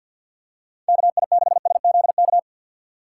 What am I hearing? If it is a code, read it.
Morse code, heard as a keyed tone, about 47 wpm, 700 Hz. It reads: KI6SZK
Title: USA Call Signs - 6 Characters - 3x Character Spacing - 50wpm